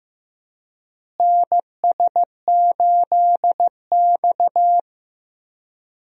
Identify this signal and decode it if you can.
Morse code, heard as a keyed tone, about 16 wpm, 700 Hz. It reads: NS8X
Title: USA Call Signs - 4 Characters - Repeated 2x - 15wpm